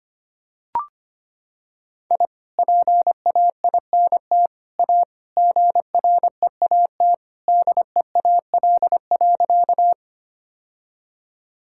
Morse code, heard as a keyed tone, about 25 wpm, 700 Hz. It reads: I PAINT A GREAT DEAL.
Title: Sentences from Top 1000 Words - Review Part 6 - 25wpm